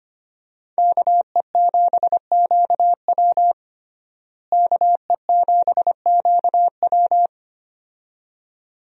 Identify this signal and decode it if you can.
Morse code, heard as a keyed tone, about 25 wpm, 700 Hz. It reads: KE7QW KE7QW
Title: USA Call Signs - 5 Characters - Repeated 2x - 25wpm